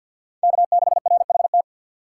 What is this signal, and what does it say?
Morse code, heard as a keyed tone, about 51 wpm, 700 Hz. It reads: K6RHT
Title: USA Call Signs - 5 Characters - 2x Character Spacing - 50wpm